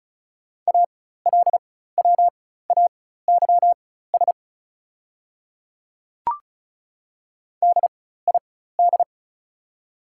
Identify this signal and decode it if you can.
Morse code, heard as a keyed tone, about 36 wpm, spaced as for 16 wpm, 690 Hz. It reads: ALWAYS DID
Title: 4x Character Spacing - Top 500 Words - 35wpm